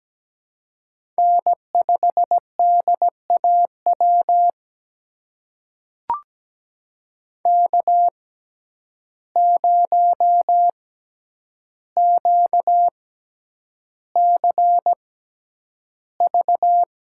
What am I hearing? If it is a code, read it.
Morse code, heard as a keyed tone, about 17 wpm, spaced as for 9 wpm, 700 Hz. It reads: N5DAW K0QCV
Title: USA Call Signs - 5 Characters - 6x Character Spacing - 17wpm